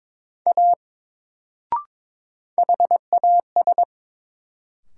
Morse code, heard as a keyed tone, about 22 wpm, 700 Hz. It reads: A HAS
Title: Sentences from Top 100 Words - New Word Drill - 22wpm